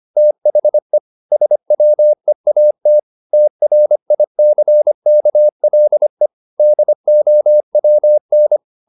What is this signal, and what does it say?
Morse code, heard as a keyed tone, about 25 wpm, 600 Hz. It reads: THE SWEAT TRICKLE DOWN
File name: the old man and the sea part 1 25wpm